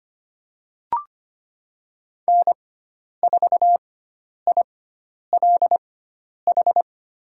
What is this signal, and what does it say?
Morse code, heard as a keyed tone, about 25 wpm, spaced as for 10 wpm, 700 Hz. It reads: N4ILH
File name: Lesson 16g - Letter c - Call Signs 25wpm